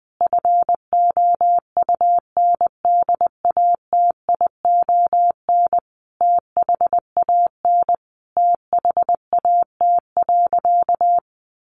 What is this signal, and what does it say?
Morse code, heard as a keyed tone, about 20 wpm, 700 Hz. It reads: FOUNDATION THAN THAT.
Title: Sentences from Top 1200 Words 20wpm